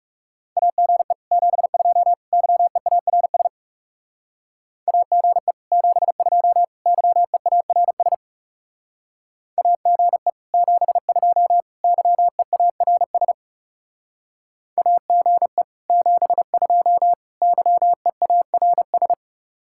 Morse code, heard as a keyed tone, about 37 wpm, 700 Hz. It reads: AGE 72 YEARS AGE 72 YEARS AGE 72 YEARS AGE 72 YEARS
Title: Speed-Racing Common QSO Phrases - 30wpm